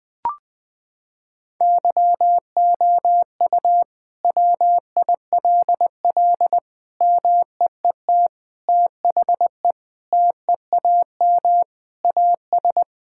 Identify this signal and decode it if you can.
Morse code, heard as a keyed tone, about 20 wpm, 700 Hz. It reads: YOU WILL MEET THE TEAM AS
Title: Sentences with 6-Word Phrases - 20wpm